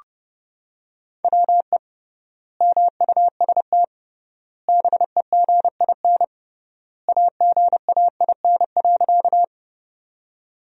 Morse code, heard as a keyed tone, about 32 wpm, 700 Hz. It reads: WE MUST BEGIN AGAIN.
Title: 3x Word-Spacing - Sentences from Top 300 Words - 30wpm